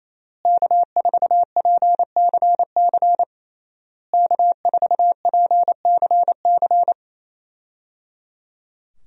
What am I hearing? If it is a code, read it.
Morse code, heard as a keyed tone, about 27 wpm, 700 Hz. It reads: K4PCC K4PCC
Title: USA Call Signs - 5 Characters - Repeated 3x - 28wpm